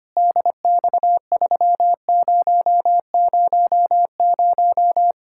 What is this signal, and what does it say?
Morse code, heard as a keyed tone, about 25 wpm, 700 Hz. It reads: DX3000
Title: Practice QSO - Exchange 2 - Info - 25wpm